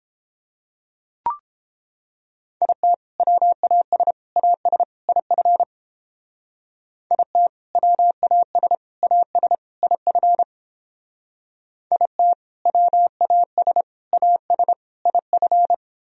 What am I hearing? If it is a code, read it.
Morse code, heard as a keyed tone, about 31 wpm, 700 Hz. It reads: IT WAS AS IF IT WAS AS IF IT WAS AS IF